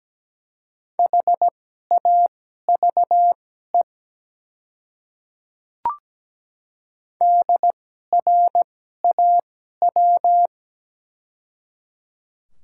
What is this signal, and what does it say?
Morse code, heard as a keyed tone, about 17 wpm, spaced as for 12 wpm, 700 Hz. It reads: HAVE DRAW
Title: 2x Character Spacing - Top 300 Words - 17wpm